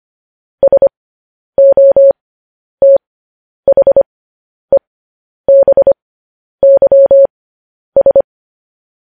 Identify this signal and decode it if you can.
Morse code, heard as a keyed tone, about 25 wpm, spaced as for 10 wpm, 550 Hz. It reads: SOTHEBYS
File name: podcast / News headlines-1x-25x10